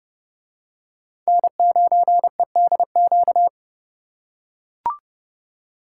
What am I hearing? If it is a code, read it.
Morse code, heard as a keyed tone, about 28 wpm, 700 Hz. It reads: N9EDQ